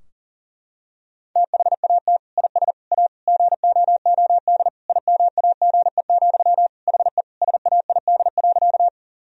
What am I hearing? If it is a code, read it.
Morse code, heard as a keyed tone, about 40 wpm, 700 Hz. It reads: THAT IS A GOOD IMAGE, HE SAID.
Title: Sentences from Top 1000 Words - Review Part 4 - 40wpm